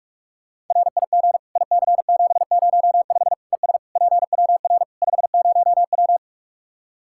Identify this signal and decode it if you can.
Morse code, heard as a keyed tone, about 46 wpm, 700 Hz. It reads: RIG IC705 ES PWR 50W
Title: Practice QSO - Exchange 2 - Info - 45wpm